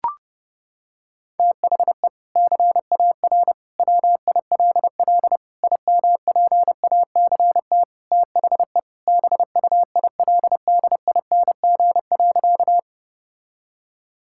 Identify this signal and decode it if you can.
Morse code, heard as a keyed tone, about 30 wpm, 700 Hz. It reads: THE CAR WILL IMPACT THE BUILDING.
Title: Sentences from Top 1000 Words - Review Part 6 - 30wpm